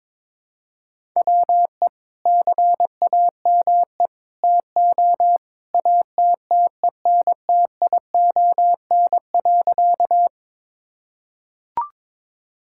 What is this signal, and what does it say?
Morse code, heard as a keyed tone, about 22 wpm, 700 Hz. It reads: WE CAME TO ATTENTION.
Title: Sentences from Top 1000 Words - Review Part 6 - 22wpm